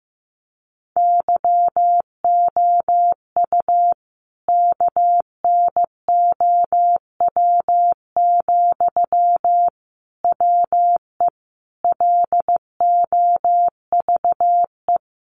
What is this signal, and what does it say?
Morse code, heard as a keyed tone, about 14 wpm, 700 Hz. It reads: YOU KNOW, WE LOVE
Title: Sentences from Top 600 Words 15wpm